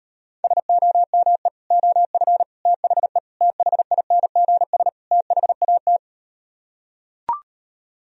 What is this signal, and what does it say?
Morse code, heard as a keyed tone, about 38 wpm, 700 Hz. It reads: SOME OF THE THINGS THAT